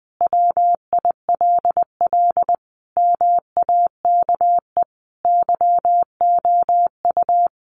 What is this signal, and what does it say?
Morse code, heard as a keyed tone, about 20 wpm, 700 Hz. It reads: WILL MAKE YOU
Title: Sentences from Top 1500 Words 20wpm